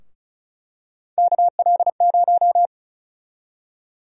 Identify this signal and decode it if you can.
Morse code, heard as a keyed tone, about 36 wpm, 700 Hz. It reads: KL0
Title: USA Call Sign Prefixes - Repeated 2x - 35wpm